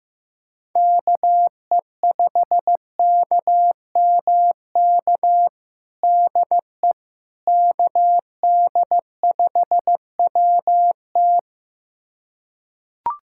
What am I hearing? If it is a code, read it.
Morse code, heard as a keyed tone, about 15 wpm, 700 Hz. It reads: KE5KMK DE KD5WT
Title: USA Callsign DE Callsign 15wpm